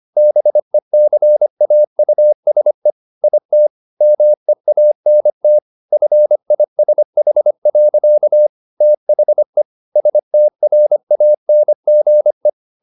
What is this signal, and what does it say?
Morse code, heard as a keyed tone, about 25 wpm, 600 Hz. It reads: BECAUSE IT MEANT FISH. THE STRANGE
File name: the old man and the sea part 1 25wpm